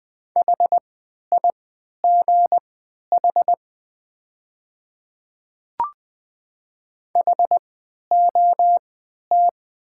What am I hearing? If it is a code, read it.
Morse code, heard as a keyed tone, about 20 wpm, spaced as for 11 wpm, 690 Hz. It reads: HIGH HOT
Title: 3x Character Spacing - Top 200 Words - 20wpm